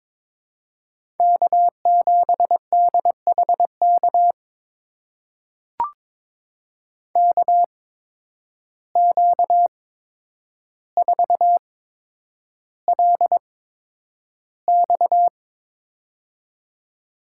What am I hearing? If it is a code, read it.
Morse code, heard as a keyed tone, about 22 wpm, spaced as for 9 wpm, 700 Hz. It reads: K7DHK KQ4LX
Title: USA Call Signs - 5 Characters - 8x Character Spacing - 22wpm